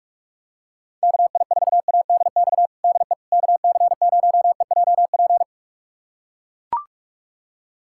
Morse code, heard as a keyed tone, about 41 wpm, 700 Hz. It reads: KI4ADX DE KC0EJP
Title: USA Callsign DE Callsign - 6 Characters - 45wpm